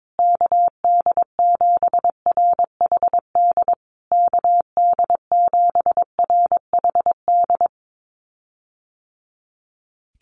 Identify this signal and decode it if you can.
Morse code, heard as a keyed tone, about 22 wpm, 700 Hz. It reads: KD7RHD KD7RHD
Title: USA Call Signs Repeated 3x 22wpm